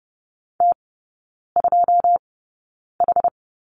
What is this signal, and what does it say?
Morse code, heard as a keyed tone, about 30 wpm, spaced as for 9 wpm, 700 Hz. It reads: T2H
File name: Lesson 16e - Letter c - Three Characters 30wpm